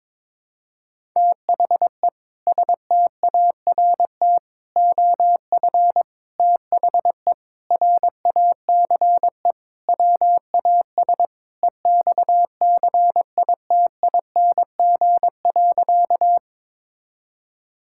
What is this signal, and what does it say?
Morse code, heard as a keyed tone, about 22 wpm, 700 Hz. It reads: THE START OF THE RACE WAS EXCITING.
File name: practice / 22wpm / Sentences with 4-Word Phrases - 22wpm